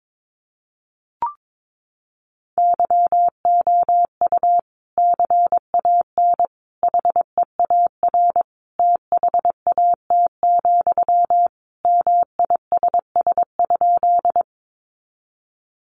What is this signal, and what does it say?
Morse code, heard as a keyed tone, about 22 wpm, 700 Hz. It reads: YOU CAN HEAR THAT, MISS?
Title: Sentences from Top 500 Words 22wpm